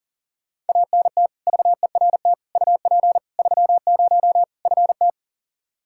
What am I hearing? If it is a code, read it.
Morse code, heard as a keyed tone, about 40 wpm, 700 Hz. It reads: ANT VERT UP 30 FT
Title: K1USN SST QSO Practice 40wpm